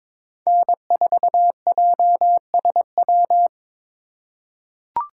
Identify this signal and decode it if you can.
Morse code, heard as a keyed tone, about 22 wpm, 700 Hz. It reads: N4JSW
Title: USA Call Signs - 2x Character Spacing - 22wpm